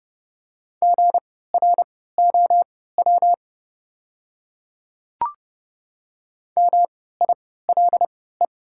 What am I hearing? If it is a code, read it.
Morse code, heard as a keyed tone, about 30 wpm, spaced as for 17 wpm, 700 Hz. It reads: GROW MILE
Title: 3x Character Spacing - Top 300 Words - 30wpm